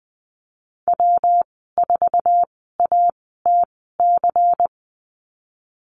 Morse code, heard as a keyed tone, about 20 wpm, spaced as for 15 wpm, 700 Hz. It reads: W4ATC
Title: 140 Lesson 23g - Letter f - Call Signs 20wpm